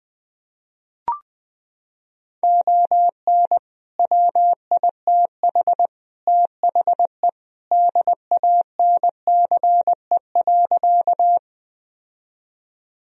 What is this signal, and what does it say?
Morse code, heard as a keyed tone, about 21 wpm, 700 Hz. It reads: ON WITH THE DANCE.
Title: Sentences from Top 1000 Words - Review Part 4 - 20wpm